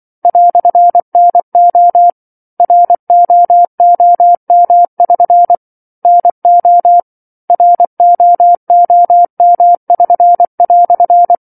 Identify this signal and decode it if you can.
Morse code, heard as a keyed tone, about 24 wpm, 700 Hz. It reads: "NO ROOM<SN> NO ROOM<SN>"
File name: 07-Chapter-07 24 WPM